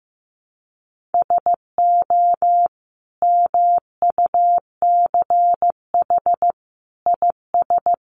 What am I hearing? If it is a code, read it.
Morse code, heard as a keyed tone, about 14 wpm, 700 Hz. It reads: SO MUCH IS
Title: Sentences from Top 1200 Words 15wpm